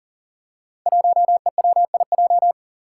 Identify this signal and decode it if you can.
Morse code, heard as a keyed tone, about 40 wpm, 700 Hz. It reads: 1EWIJ
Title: Five Challenging Letter-Number Combinations 40wpm